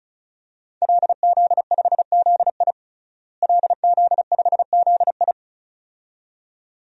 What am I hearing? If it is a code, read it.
Morse code, heard as a keyed tone, about 36 wpm, 700 Hz. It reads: LZ5ZI LZ5ZI
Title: DXCC-100 Call Signs - 5 Characters - Repeated 2x - 35wpm